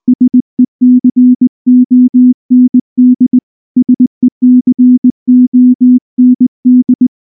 Morse code, heard as a keyed tone, about 19 wpm, 250 Hz. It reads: SECOND SECOND
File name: morse / morse-training-20WPM-7